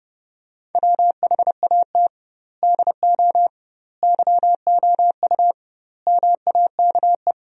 Morse code, heard as a keyed tone, about 32 wpm, 700 Hz. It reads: WHAT DO YOU MAKE
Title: 2x Word-Spacing - Sentences from Top 100 Words - 30wpm